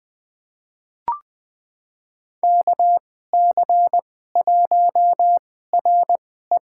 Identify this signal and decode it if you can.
Morse code, heard as a keyed tone, about 21 wpm, spaced as for 15 wpm, 700 Hz. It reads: KC1RE
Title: USA Call Signs - 5 Characters - 2x Character Spacing - 20wpm